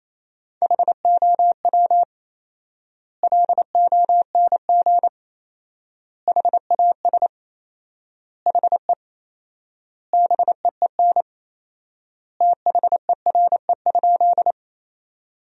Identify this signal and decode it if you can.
Morse code, heard as a keyed tone, about 28 wpm, 700 Hz. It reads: HOW LONG HAS HE BEEN THERE?
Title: 4x Word-Spacing - Sentences from Top 100 Words - 28wpm